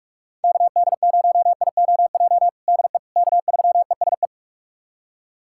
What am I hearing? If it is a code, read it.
Morse code, heard as a keyed tone, about 46 wpm, 700 Hz. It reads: KD0IOJ DE K3ESE
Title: USA Callsign DE Callsign 45wpm